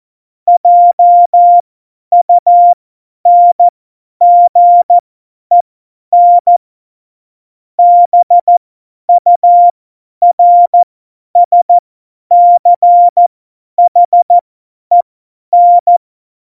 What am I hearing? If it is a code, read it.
Morse code, heard as a keyed tone, about 14 wpm, spaced as for 10 wpm, 700 Hz. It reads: JUNGEN BURSCHEN
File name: Morsekurs / Witze / Witz 08. Dezember 2020, 50 F70